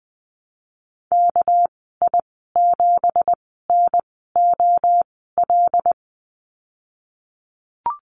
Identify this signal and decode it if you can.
Morse code, heard as a keyed tone, about 20 wpm, spaced as for 15 wpm, 700 Hz. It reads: KI7NOL
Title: 231 Lesson 37g - Letter x - Call Signs 20wpm